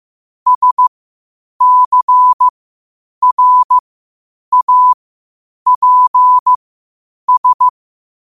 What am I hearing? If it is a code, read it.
Morse code, heard as a keyed tone, about 15 wpm, spaced as for 8 wpm, 1000 Hz. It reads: SCRAPS
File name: qotdfiles / 8wpm